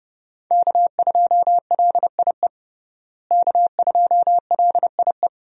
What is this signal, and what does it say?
Morse code, heard as a keyed tone, about 32 wpm, 700 Hz. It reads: K2LIE K2LIE
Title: USA Call Signs - 5 Characters - Repeated 2x - 30wpm